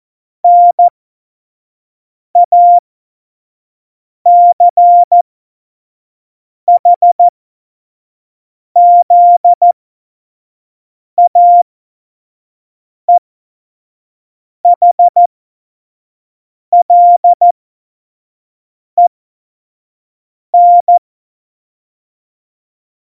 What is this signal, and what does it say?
Morse code, heard as a keyed tone, about 14 wpm, spaced as for 5 wpm, 700 Hz. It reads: NACHZAEHLEN
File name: Morsekurs / Witze / Witz 17. September 2019, 25 F70